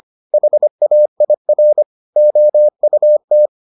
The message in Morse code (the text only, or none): HAIR OUT